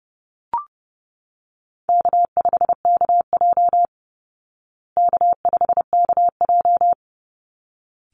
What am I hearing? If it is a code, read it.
Morse code, heard as a keyed tone, about 30 wpm, 700 Hz. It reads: K5KJ K5KJ
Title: Call Signs Repeated 3x - 30wpm